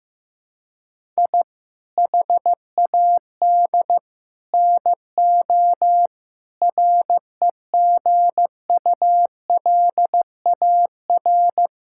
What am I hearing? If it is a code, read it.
Morse code, heard as a keyed tone, about 15 wpm, 700 Hz. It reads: I HAD NO REGULAR
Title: Sentences from Top 1000 Words - Review Part 6 - 15wpm